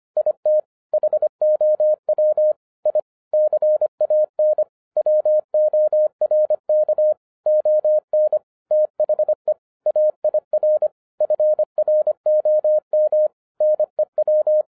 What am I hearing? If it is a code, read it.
Morse code, heard as a keyed tone, about 26 wpm, 610 Hz. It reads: IT HOW I CAN WORK ON THE AIR FROM NEW
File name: 2011-10-mp3 / n04-JUST MOVED 25wpm